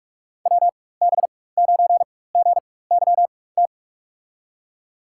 Morse code, heard as a keyed tone, about 46 wpm, spaced as for 21 wpm, 700 Hz. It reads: WB9GYT
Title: USA Call Signs - 4x Character Spacing - 45wpm